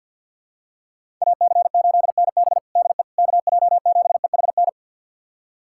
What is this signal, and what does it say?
Morse code, heard as a keyed tone, about 52 wpm, 700 Hz. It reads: AC8NB DE KJ7EHN
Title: USA Callsign DE Callsign 50wpm